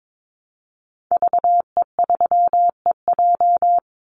Morse code, heard as a keyed tone, about 22 wpm, 700 Hz. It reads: VE3EJ
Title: Call Signs Repeated 3x - 22wpm